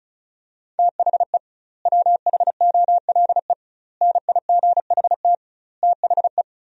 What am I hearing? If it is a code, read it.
Morse code, heard as a keyed tone, about 36 wpm, 700 Hz. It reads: THE WHOLE NIGHT THE
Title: 2x Word-Spacing - Sentences from Top 500 Words - 35wpm